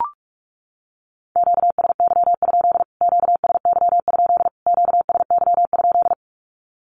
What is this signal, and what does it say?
Morse code, heard as a keyed tone, about 44 wpm, 700 Hz. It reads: QSY? QSY? QSY?
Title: Q-Codes Repeated 3x 45wpm